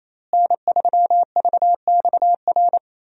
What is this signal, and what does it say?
Morse code, heard as a keyed tone, about 28 wpm, 700 Hz. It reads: N3VXR
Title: USA Call Signs - 5 Characters - Repeated 3x - 28wpm